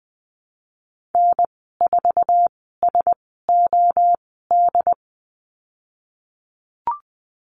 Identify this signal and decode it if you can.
Morse code, heard as a keyed tone, about 20 wpm, spaced as for 15 wpm, 700 Hz. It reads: N4SOD
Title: 076 Lesson 13g - Letter L - Call Signs 20wpm